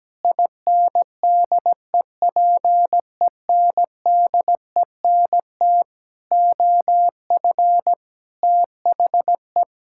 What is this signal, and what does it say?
Morse code, heard as a keyed tone, about 17 wpm, 700 Hz. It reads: INDEPENDENT OF THE
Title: Sentences from Top 1000 Words - Review Part 6 - 17wpm